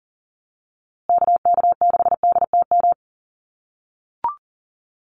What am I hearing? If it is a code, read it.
Morse code, heard as a keyed tone, about 40 wpm, 700 Hz. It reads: KK6DTM
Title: USA Call Signs Repeated 3x 40wpm